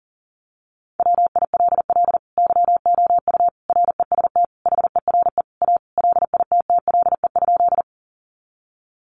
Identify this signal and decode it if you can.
Morse code, heard as a keyed tone, about 40 wpm, 700 Hz. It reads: WILL YOU REST HERE A LITTLE?
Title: Sentences from Top 500 Words 40wpm